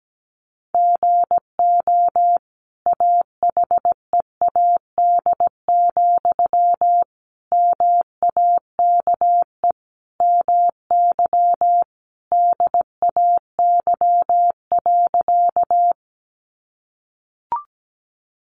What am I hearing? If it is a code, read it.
Morse code, heard as a keyed tone, about 17 wpm, 700 Hz. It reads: GO AHEAD, MAKE MY DAY.
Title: Sentences from Top 1100 Words 17wpm